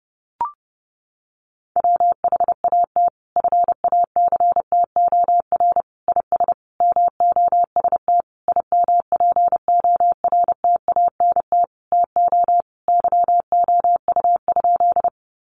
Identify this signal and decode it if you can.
Morse code, heard as a keyed tone, about 30 wpm, 700 Hz. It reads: WHAT FACTOR IS MOST IMPORTANT TO YOU?
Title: Sentences from Top 1300 Words 30wpm